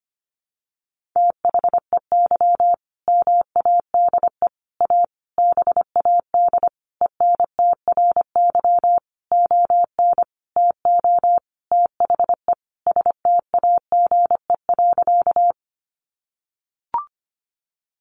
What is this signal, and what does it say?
Morse code, heard as a keyed tone, about 25 wpm, 700 Hz. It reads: THEY MADE A BAD ENTRY ON TO THE STAGE.
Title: Sentences from Top 1500 Words 25wpm